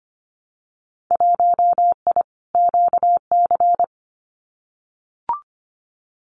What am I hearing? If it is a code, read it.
Morse code, heard as a keyed tone, about 25 wpm, 700 Hz. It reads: 1I QC
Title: Winter Field Day Exchanges 25wpm